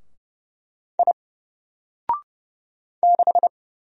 Morse code, heard as a keyed tone, about 30 wpm, 700 Hz. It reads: I 6
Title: Single Character (Letter:Number:Punctuation:Prosign) 30wpm